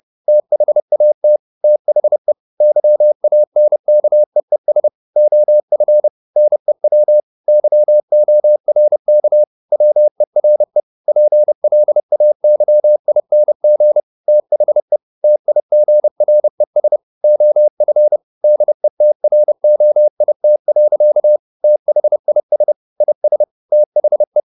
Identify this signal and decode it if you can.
Morse code, heard as a keyed tone, about 30 wpm, 600 Hz. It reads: THAT THE YANKEES OF NEW YORK WERE PLAYING THE TIGRES OF DETROIT. THIS IS THE
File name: the old man and the sea / the old man and the sea 1 30wpm